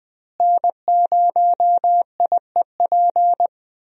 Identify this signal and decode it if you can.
Morse code, heard as a keyed tone, about 20 wpm, 700 Hz. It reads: N0IEP
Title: USA Call Signs - 5 Characters - Repeated 3x - 20wpm